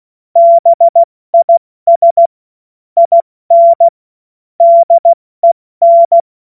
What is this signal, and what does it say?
Morse code, heard as a keyed tone, about 16 wpm, 680 Hz. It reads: BIS IN DEN
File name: Morsekurs / Witze / Witz 09. Dezember 2025 70 F80